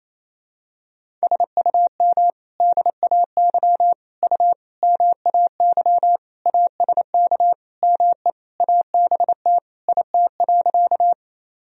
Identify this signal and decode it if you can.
Morse code, heard as a keyed tone, about 28 wpm, 700 Hz. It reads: SUM DAY U MAY ASK ME ABT IT.
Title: Sentences from Top 200 Words and Abbreviations 28wpm